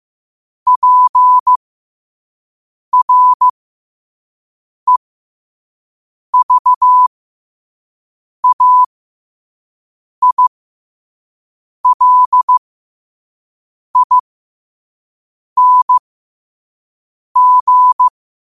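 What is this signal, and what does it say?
Morse code, heard as a keyed tone, about 15 wpm, spaced as for 5 wpm, 1000 Hz. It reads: PREVAILING
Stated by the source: Morse code sent at 5 wpm.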